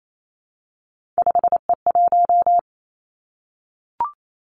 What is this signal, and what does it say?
Morse code, heard as a keyed tone, about 28 wpm, 700 Hz. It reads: 5E1
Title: Three Challenging Letter-Number Combinations 28wpm